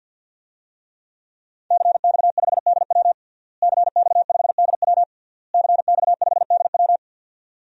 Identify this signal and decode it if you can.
Morse code, heard as a keyed tone, about 50 wpm, 700 Hz. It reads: CX5DW CX5DW CX5DW
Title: DXCC-100 Call Signs - 5 Characters - Repeated 3x - 50wpm